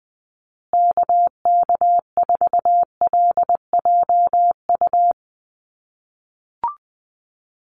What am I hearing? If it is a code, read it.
Morse code, heard as a keyed tone, about 20 wpm, 700 Hz. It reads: KK4LJU
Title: USA Call Signs Repeated 2x 20wpm